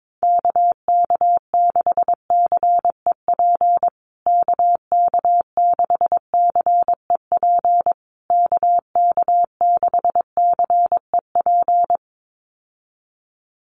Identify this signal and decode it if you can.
Morse code, heard as a keyed tone, about 22 wpm, 700 Hz. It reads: KK6CEP KK6CEP KK6CEP
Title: USA Call Signs Repeated 3x 22wpm